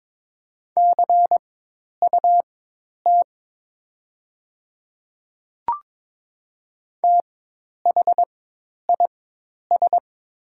Morse code, heard as a keyed tone, about 22 wpm, spaced as for 10 wpm, 700 Hz. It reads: CUT THIS
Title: 4x Character Spacing - Top 300 Words - 22wpm